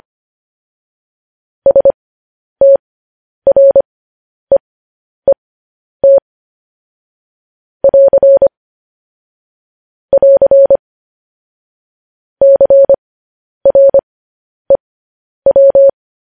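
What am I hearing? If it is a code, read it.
Morse code, heard as a keyed tone, about 25 wpm, spaced as for 10 wpm, 550 Hz. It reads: STREET <AR> <AR> CREW
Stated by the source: Morse code news headlines sent at 10 wpm.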